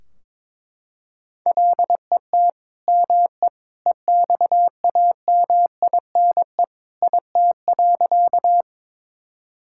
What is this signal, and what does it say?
Morse code, heard as a keyed tone, about 22 wpm, 700 Hz. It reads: LET ME EXAMINE IT.
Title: Sentences from Top 1600 Words 22wpm